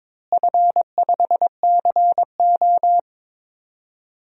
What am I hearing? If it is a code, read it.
Morse code, heard as a keyed tone, about 22 wpm, 700 Hz. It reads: F5CO
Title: DXCC-100 Call Signs - 4 Characters - Repeated 2x - 22wpm